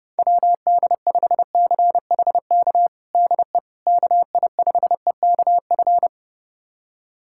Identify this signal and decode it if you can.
Morse code, heard as a keyed tone, about 30 wpm, 700 Hz. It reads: WD5CHK DE KI5EKF
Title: USA Callsign DE Callsign 30wpm